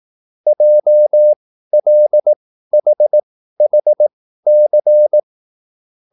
Morse code, heard as a keyed tone, about 18 wpm, spaced as for 13 wpm, 600 Hz. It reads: JLHHC
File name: QSL / CW lesson